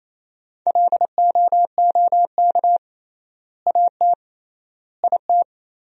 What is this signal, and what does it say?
Morse code, heard as a keyed tone, about 28 wpm, 700 Hz. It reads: LOOK AT IT
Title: 3x Word-Spacing - Sentences from Top 200 Words - 28wpm